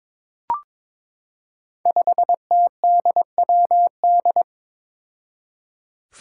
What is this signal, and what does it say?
Morse code, heard as a keyed tone, about 22 wpm, 700 Hz. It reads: 5TDWD
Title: Five Challenging Letter-Number Combinations 22wpm